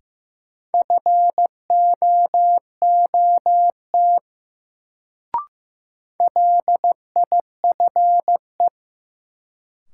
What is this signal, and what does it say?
Morse code, heard as a keyed tone, about 15 wpm, 700 Hz. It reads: FOOT LIFE
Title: Sets of 5 Words - New Word Drill - 15wpm